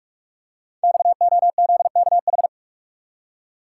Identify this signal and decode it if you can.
Morse code, heard as a keyed tone, about 44 wpm, 700 Hz. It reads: XOZKH
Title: Code Groups - 5 Letter:Numbers - 45wpm